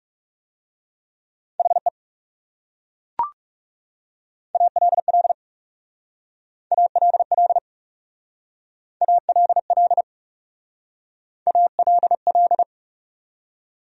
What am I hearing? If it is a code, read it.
Morse code, heard as a keyed tone, about 38 wpm, 700 Hz. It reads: HE ALL ALL ALL ALL